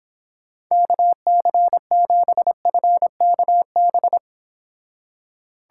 Morse code, heard as a keyed tone, about 26 wpm, 700 Hz. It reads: KC7FKB